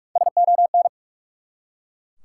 Morse code, heard as a keyed tone, about 44 wpm, 700 Hz. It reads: SON
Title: Three Letter Words - 45wpm